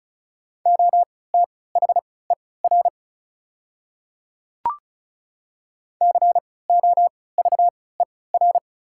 Morse code, heard as a keyed tone, about 36 wpm, spaced as for 20 wpm, 690 Hz. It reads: OTHER COVER
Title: 3x Character Spacing - Top 300 Words - 35wpm